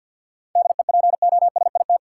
Morse code, heard as a keyed tone, about 52 wpm, 700 Hz. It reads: DEPOSIT
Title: Sentences from Top 1300 Words - New Word Drill Alt - 50wpm